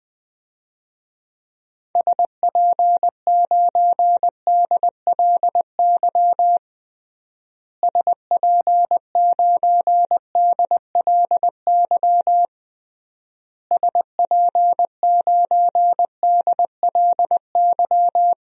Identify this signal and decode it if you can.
Morse code, heard as a keyed tone, about 20 wpm, 700 Hz. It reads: SP9DLY SP9DLY SP9DLY
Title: DXCC-100 Call Signs - 6 Characters - Repeated 3x - 20wpm